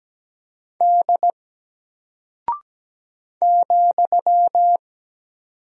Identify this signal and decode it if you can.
Morse code, heard as a keyed tone, about 17 wpm, 700 Hz. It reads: D ,
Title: Single Character (Letter:Number:Punctuation:Prosign) Alt 17wpm